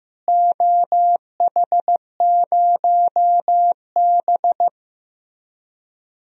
Morse code, heard as a keyed tone, about 15 wpm, 700 Hz. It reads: OH0B
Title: DXCC-100 Call Signs - 4 Characters - Repeated 3x - 15wpm